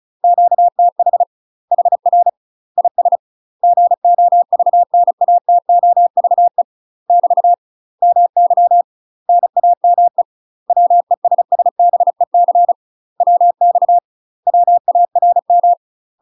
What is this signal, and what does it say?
Morse code, heard as a keyed tone, about 34 wpm, 700 Hz. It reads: QTH HR IS GOVNATOVE <BT> MY NAME WESSBEC WX WARM
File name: UK MASTER QSOs TRAINING / 35WPM / UK MASTER QSOs 0037